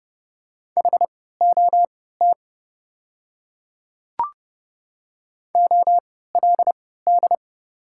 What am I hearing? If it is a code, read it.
Morse code, heard as a keyed tone, about 30 wpm, spaced as for 17 wpm, 700 Hz. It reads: HOT OLD
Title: 3x Character Spacing - Top 300 Words - 30wpm